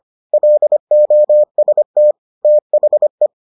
Morse code, heard as a keyed tone, about 25 wpm, 600 Hz. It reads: LOST THE